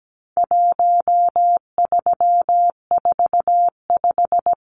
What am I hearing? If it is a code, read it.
Morse code, heard as a keyed tone, about 17 wpm, 700 Hz. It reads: 1345
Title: Numbers - 4-Digit 17wpm